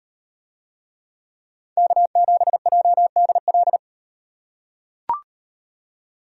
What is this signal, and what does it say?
Morse code, heard as a keyed tone, about 38 wpm, 700 Hz. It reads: K7JDL